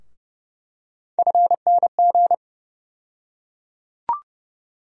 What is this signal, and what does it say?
Morse code, heard as a keyed tone, about 31 wpm, 700 Hz. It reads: FNG
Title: USA Call Sign Postfixes - 3 Letters 30wpm